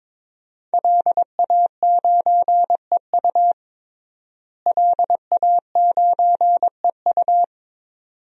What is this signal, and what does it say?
Morse code, heard as a keyed tone, about 22 wpm, 700 Hz. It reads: LA9EU LA9EU
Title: DXCC-100 Call Signs - 5 Characters - Repeated 2x - 22wpm